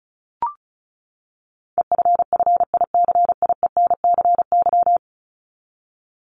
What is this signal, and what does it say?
Morse code, heard as a keyed tone, about 34 wpm, 700 Hz. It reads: EFFICIENCY
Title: Top 1501-1750 Words 35wpm